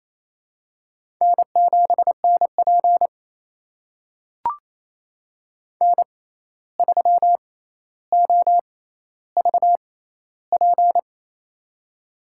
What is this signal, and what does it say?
Morse code, heard as a keyed tone, about 27 wpm, spaced as for 13 wpm, 700 Hz. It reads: N7NP N3OVP
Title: USA Call Signs - 6x Character Spacing - 28wpm